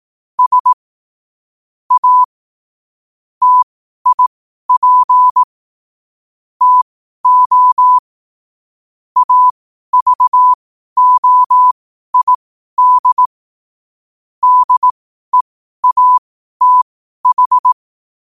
Morse code, heard as a keyed tone, about 18 wpm, spaced as for 12 wpm, 1000 Hz. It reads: S A TIP TO AVOID DEATH
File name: qotdfiles / 12wpm